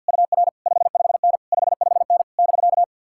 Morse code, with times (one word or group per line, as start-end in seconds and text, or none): 0.1-0.5 UR
0.7-1.4 55N
1.5-2.2 55N
2.4-2.8 <BK>